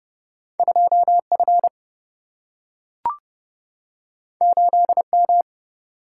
Morse code, heard as a keyed tone, about 30 wpm, 700 Hz. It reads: 2 F 8 M
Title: Single-Letter Single-Number 30wpm